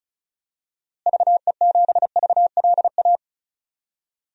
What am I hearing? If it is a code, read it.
Morse code, heard as a keyed tone, about 34 wpm, 700 Hz. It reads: VE7VLA
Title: DXCC-100 Call Signs - 6 Characters - Repeated 2x - 35wpm